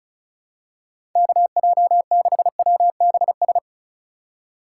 Morse code, heard as a keyed tone, about 34 wpm, 700 Hz. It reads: KJ6WBS
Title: USA Call Signs - 3x Character Spacing - 35wpm